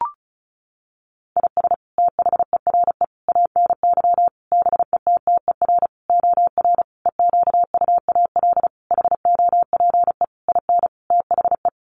Morse code, heard as a keyed tone, about 34 wpm, 700 Hz. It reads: IS THERE ANY BETTER OR EQUAL HOPE IN THE
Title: Sentences from Top 1400 Words 35wpm